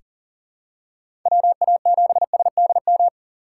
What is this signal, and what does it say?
Morse code, heard as a keyed tone, about 40 wpm, 700 Hz. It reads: WA7SDM
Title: USA Call Signs - 6 Characters - 4x Character Spacing - 40wpm